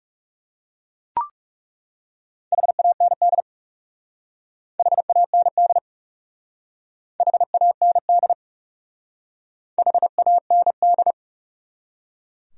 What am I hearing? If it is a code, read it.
Morse code, heard as a keyed tone, about 35 wpm, 700 Hz. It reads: HAND HAND HAND HAND
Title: Speed-Racing Top 300 Words 30wpm